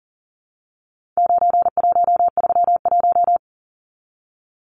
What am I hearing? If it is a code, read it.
Morse code, heard as a keyed tone, about 40 wpm, 700 Hz. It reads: 9131
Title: Numbers - 4-Digit 40wpm